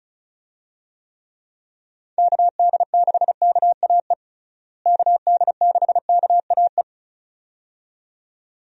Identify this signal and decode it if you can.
Morse code, heard as a keyed tone, about 36 wpm, 700 Hz. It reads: KD6KAE KD6KAE
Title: USA Call Signs - 6 Characters - Repeated 2x - 35wpm